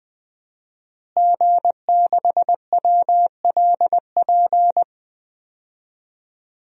Morse code, heard as a keyed tone, about 21 wpm, 700 Hz. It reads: G6WLP
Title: DXCC-100 Call Signs - 5 Characters - Repeated 2x - 20wpm